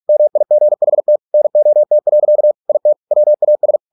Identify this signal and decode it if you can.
Morse code, heard as a keyed tone, about 46 wpm, 600 Hz. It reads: MIGHT NOT. IT WAS